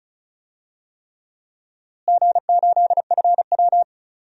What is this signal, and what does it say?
Morse code, heard as a keyed tone, about 36 wpm, 700 Hz. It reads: G8FW
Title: DXCC-100 Call Signs - 4 Characters - Repeated 3x - 35wpm